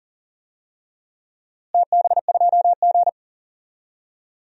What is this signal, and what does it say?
Morse code, heard as a keyed tone, about 40 wpm, 700 Hz. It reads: TB2G
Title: DXCC-100 Call Signs - 4 Characters - Repeated 3x - 40wpm